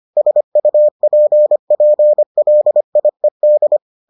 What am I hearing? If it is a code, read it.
Morse code, heard as a keyed tone, about 25 wpm, 600 Hz. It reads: SUPPLIED